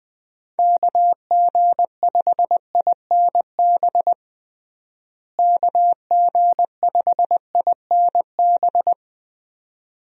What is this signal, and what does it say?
Morse code, heard as a keyed tone, about 20 wpm, 700 Hz. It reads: KG5INB KG5INB
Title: USA Call Signs - 6 Characters - Repeated 2x - 20wpm